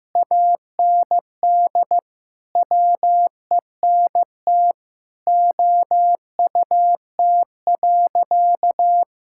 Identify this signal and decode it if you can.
Morse code, heard as a keyed tone, about 15 wpm, 700 Hz. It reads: AND WENT OUT.
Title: Sentences from Top 1600 Words 15wpm